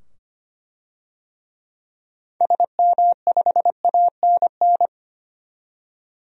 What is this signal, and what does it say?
Morse code, heard as a keyed tone, about 25 wpm, 700 Hz. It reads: SM5ANN
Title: DXCC-100 Call Signs - 6 Characters - 25wpm